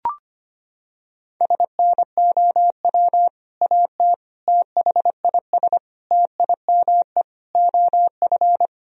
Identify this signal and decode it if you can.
Morse code, heard as a keyed tone, about 25 wpm, 700 Hz. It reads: SNOW AT THIS TIME OF
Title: Sentences from Top 1000 Words - Review Part 3 - 25wpm